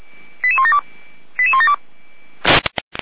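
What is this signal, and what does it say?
Five-tone selective call (ZVEI), heard as quick runs of five tones, tones of about 70 ms. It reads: 88172 88172
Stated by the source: ZVEI five-tone call received by SDR.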